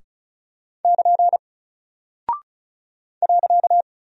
Morse code, heard as a keyed tone, about 33 wpm, 700 Hz. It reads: <KN> .
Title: Single Character (Letter:Number:Punctuation:Prosign) Alt 35wpm